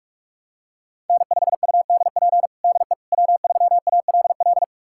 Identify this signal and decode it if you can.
Morse code, heard as a keyed tone, about 42 wpm, 700 Hz. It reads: N5UDP DE W3ALL
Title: USA Callsign DE Callsign - 5 Characters - 45wpm